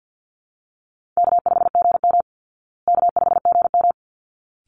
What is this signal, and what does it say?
Morse code, heard as a keyed tone, about 48 wpm, 700 Hz. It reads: K5GM K5GM
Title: Call Signs Repeated 3x - 50wpm